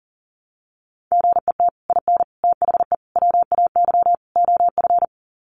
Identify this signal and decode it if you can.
Morse code, heard as a keyed tone, about 40 wpm, 700 Hz. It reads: GET IN THE WAY OF
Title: Sets of 5 Words Encore 40wpm